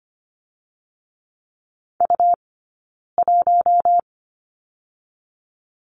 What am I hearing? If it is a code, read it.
Morse code, heard as a keyed tone, about 25 wpm, 700 Hz. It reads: U 1
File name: Lesson 17d - Letter u - Two Characters 25wpm